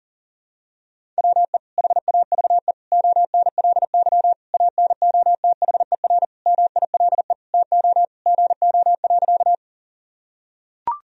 Morse code, heard as a keyed tone, about 40 wpm, 700 Hz. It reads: WE HAVE ONLY ANOTHER MILE TO GO.
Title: Sentences from Top 1000 Words - Review Part 1 - 40wpm